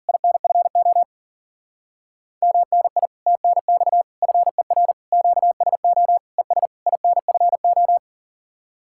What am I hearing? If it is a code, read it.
Morse code, heard as a keyed tone, about 51 wpm, 700 Hz. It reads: INFO MNI TNX FER QSO ES INFO